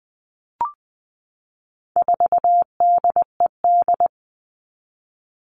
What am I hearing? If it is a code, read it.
Morse code, heard as a keyed tone, about 20 wpm, 700 Hz. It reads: 4DED
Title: All Cut Numbers - 4-Digit 20wpm